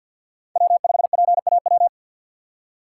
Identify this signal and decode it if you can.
Morse code, heard as a keyed tone, about 53 wpm, 700 Hz. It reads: W5PAW
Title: USA Call Signs - 5 Characters - 2x Character Spacing - 50wpm